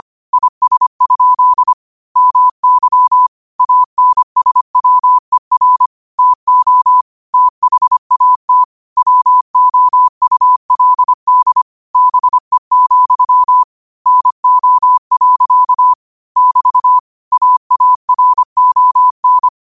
Morse code, heard as a keyed tone, about 25 wpm, 1000 Hz. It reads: IS? MY ANSWER TO THAT WOULD BE, NO. <BT> AARON
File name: qotdfiles / 25wpm